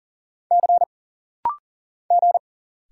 Morse code, heard as a keyed tone, about 40 wpm, 700 Hz. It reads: C G
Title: Single Letters - Rapid-Fire 40wpm